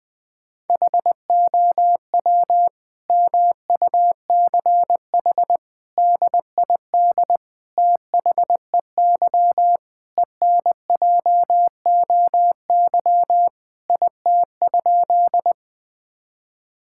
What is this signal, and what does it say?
Morse code, heard as a keyed tone, about 20 wpm, 700 Hz. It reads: HOW MUCH DID THEY ENJOY IT?
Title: Sentences from Top 1000 Words - Review Part 5 - 20wpm